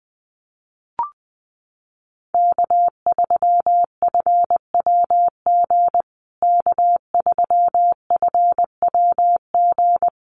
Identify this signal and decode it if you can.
Morse code, heard as a keyed tone, about 20 wpm, 700 Hz. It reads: K3FWG K3FWG
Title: USA Call Signs Repeated 3x 20wpm